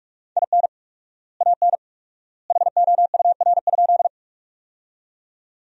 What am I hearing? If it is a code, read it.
Morse code, heard as a keyed tone, about 46 wpm, 700 Hz. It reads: IN AN HOUR?
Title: 4x Word-Spacing - Sentences from Top 400 Words - 45wpm